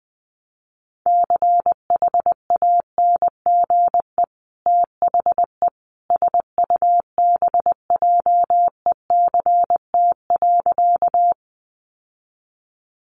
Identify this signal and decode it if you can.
Morse code, heard as a keyed tone, about 20 wpm, 700 Hz. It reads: CHANGE THE SUBJECT.
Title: Sentences from Top 800 Words 20wpm